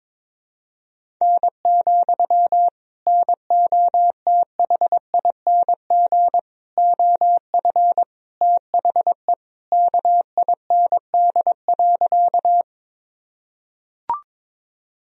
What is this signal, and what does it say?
Morse code, heard as a keyed tone, about 21 wpm, 700 Hz. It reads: N, NOTHING OF THE KIND.
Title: Sentences from Top 500 Words and Abbreviations 22wpm